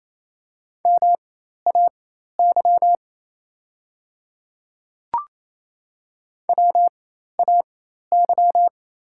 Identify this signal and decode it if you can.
Morse code, heard as a keyed tone, about 28 wpm, spaced as for 13 wpm, 690 Hz. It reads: MAY WAY
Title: 4x Character Spacing - Top 200 Words - 28wpm